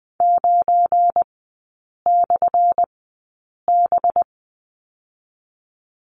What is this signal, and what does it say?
Morse code, heard as a keyed tone, about 20 wpm, 700 Hz. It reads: 9 / B
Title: 212 Lesson 34e - Number 8 - Three Characters 20wpm